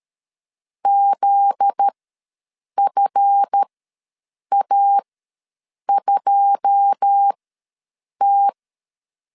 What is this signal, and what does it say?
Morse code, heard as a keyed tone, about 13 wpm, spaced as for 7 wpm, 800 Hz. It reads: ZFA2T
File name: LEZIONI / 9 2 ws 12wpm 3406